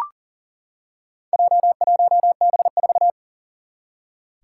Morse code, heard as a keyed tone, about 40 wpm, 700 Hz. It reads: J1B4
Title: Four Challenging Letter-Number Combinations 40wpm